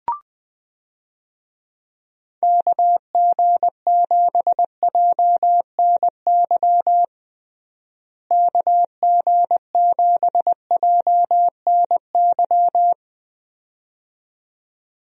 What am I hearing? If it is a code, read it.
Morse code, heard as a keyed tone, about 20 wpm, 700 Hz. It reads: KG7JNY KG7JNY
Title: USA Call Signs - 6 Characters - Repeated 2x - 20wpm